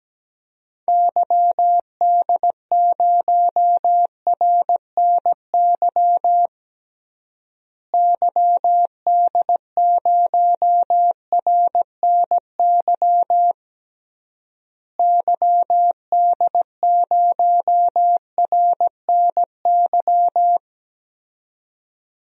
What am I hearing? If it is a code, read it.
Morse code, heard as a keyed tone, about 17 wpm, 700 Hz. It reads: YD0RNY YD0RNY YD0RNY
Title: DXCC-100 Call Signs - 6 Characters - Repeated 3x - 17wpm